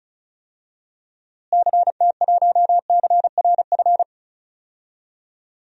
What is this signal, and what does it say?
Morse code, heard as a keyed tone, about 36 wpm, 700 Hz. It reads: CT1CRF
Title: DXCC-100 Call Signs - 6 Characters - Repeated 2x - 35wpm